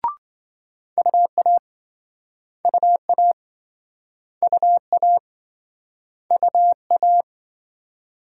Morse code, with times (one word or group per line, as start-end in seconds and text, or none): 1.0-1.6 UA
2.6-3.3 UA
4.4-5.2 UA
6.3-7.2 UA